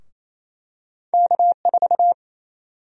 Morse code, heard as a keyed tone, about 28 wpm, 700 Hz. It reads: K4
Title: USA Call Sign Prefixes - Repeated 2x - 28wpm